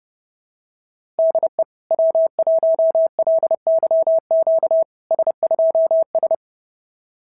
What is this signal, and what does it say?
Morse code, heard as a keyed tone, about 30 wpm, 690 Hz. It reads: DE W1LYQ S2S
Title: SOTA QSO Practice 30wpm